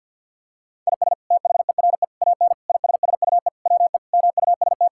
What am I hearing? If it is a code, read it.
Morse code, heard as a keyed tone, about 50 wpm, 700 Hz. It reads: IS THERE AN ISSUE WE MUST
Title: Sentences from Top 1000 Words - Review Part 6 - 50wpm